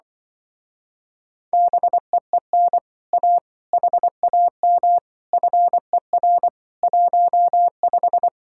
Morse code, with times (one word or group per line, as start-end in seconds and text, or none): 1.5-2.8 BEEN
3.1-3.4 A
3.7-5.0 HAM
5.3-6.5 FER
6.8-8.3 15